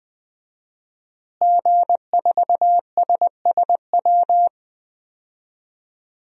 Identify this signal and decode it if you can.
Morse code, heard as a keyed tone, about 20 wpm, 700 Hz. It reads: G4SSW
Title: DXCC-100 Call Signs - 5 Characters - Repeated 3x - 20wpm